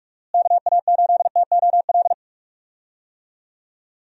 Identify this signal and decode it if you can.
Morse code, heard as a keyed tone, about 46 wpm, 700 Hz. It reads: KA8TOL
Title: USA Call Signs - 6 Characters - Repeated 3x - 45wpm